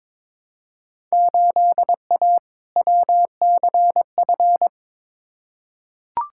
Morse code, heard as a keyed tone, about 22 wpm, 700 Hz. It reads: 8A WCF
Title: Field Day Exchanges Encore - 22wpm